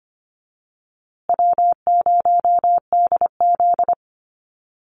Morse code, heard as a keyed tone, about 25 wpm, 700 Hz. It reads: W0DZ
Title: USA Call Signs Repeated 2x 25wpm